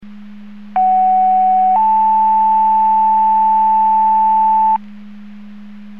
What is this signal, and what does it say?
Two-tone page, heard as a one-second tone, then a three-second tone, about 750 then 880 Hz. It is heard from 0.8 to 4.8 s.
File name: Portage 154.385 Paging